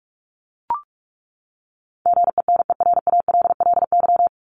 Morse code, heard as a keyed tone, about 44 wpm, 700 Hz. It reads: GENERALLY
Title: Top 501-750 Words 45wpm